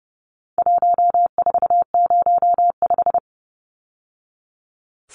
Morse code, heard as a keyed tone, about 28 wpm, 700 Hz. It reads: 1405
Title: Numbers - 4-Digit 30wpm